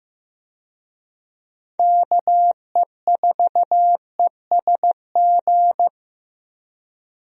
Morse code, heard as a keyed tone, about 15 wpm, 700 Hz. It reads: KE4ESG
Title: USA Call Signs - 6 Characters - Repeated 3x - 15wpm